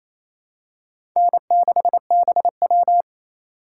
Morse code, heard as a keyed tone, about 28 wpm, 700 Hz. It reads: N6BW
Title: USA Call Signs - 4 Characters - 3x Character Spacing - 28wpm